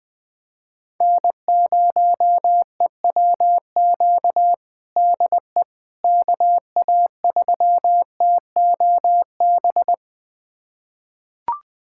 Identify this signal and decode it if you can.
Morse code, heard as a keyed tone, about 20 wpm, 700 Hz. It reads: N0EWQ DE KA3TOB
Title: USA Callsign DE Callsign 20wpm